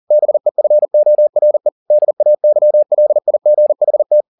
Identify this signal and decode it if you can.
Morse code, heard as a keyed tone, about 40 wpm, 600 Hz. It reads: BEFORE DAYLIGHT